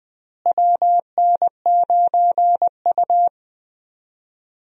Morse code, heard as a keyed tone, about 20 wpm, 700 Hz. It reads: WN9U
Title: USA Call Signs - 4 Characters - Repeated 2x - 20wpm